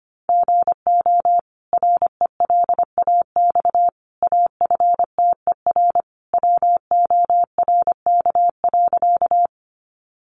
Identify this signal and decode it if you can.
Morse code, heard as a keyed tone, about 25 wpm, 700 Hz. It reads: GO RELAX AFTER WORK.
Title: Sentences from Top 1400 Words 25wpm